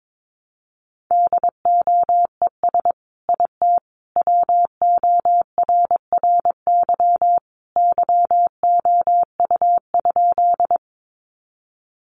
Morse code, heard as a keyed tone, about 22 wpm, 700 Hz. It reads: DOES IT WORRY YOU?
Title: Sentences from Top 1100 Words 22wpm